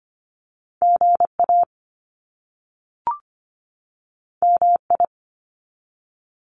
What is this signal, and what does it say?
Morse code, heard as a keyed tone, about 25 wpm, 700 Hz. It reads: GA MI
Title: US States 25wpm